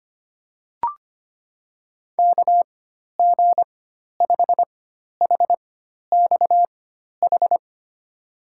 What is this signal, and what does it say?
Morse code, heard as a keyed tone, about 25 wpm, spaced as for 12 wpm, 700 Hz. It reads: KG5HXH
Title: USA Call Signs - 6 Characters - 4x Character Spacing - 25wpm